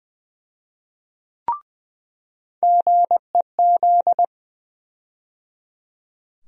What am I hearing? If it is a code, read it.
Morse code, heard as a keyed tone, about 20 wpm, 700 Hz. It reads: GEZ
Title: USA Call Sign Postfixes - 3 Letters 20wpm